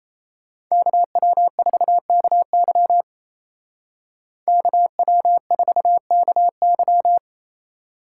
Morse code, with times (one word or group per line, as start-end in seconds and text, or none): 0.7-3.0 KW4KY
4.5-7.2 KW4KY